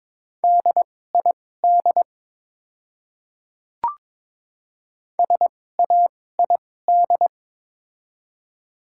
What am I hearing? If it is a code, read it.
Morse code, heard as a keyed tone, about 22 wpm, spaced as for 16 wpm, 700 Hz. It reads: DID SAID
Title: 2x Character Spacing - Top 100 Words - 22wpm